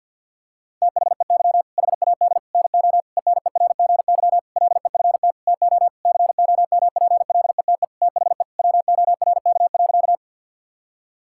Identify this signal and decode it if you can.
Morse code, heard as a keyed tone, about 52 wpm, 700 Hz. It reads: THEY HAD NO ENERGY LEFT TO COMPLETE THE WORK.
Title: Sentences from Top 1000 Words - Review Part 6 - 50wpm